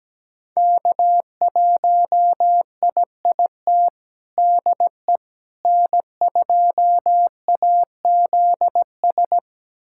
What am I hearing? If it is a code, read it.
Morse code, heard as a keyed tone, about 17 wpm, 700 Hz. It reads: K1IIT DE N2AZS
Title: USA Callsign DE Callsign - 5 Characters - 17wpm